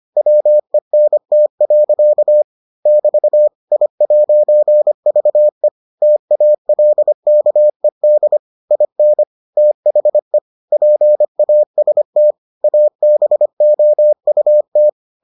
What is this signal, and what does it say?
Morse code, heard as a keyed tone, about 25 wpm, 600 Hz. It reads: WENT. <BT> I'VE TALKED IN THE PAST ABOUT